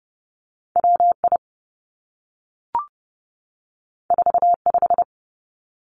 Morse code, heard as a keyed tone, about 30 wpm, 700 Hz. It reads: W I 4 5
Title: Two Challenging Letter-Number Combinations 30wpm